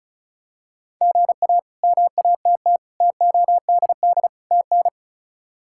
Morse code, heard as a keyed tone, about 34 wpm, 700 Hz. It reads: GA MATT TODD TN
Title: K1USN SST QSO Practice - 35wpm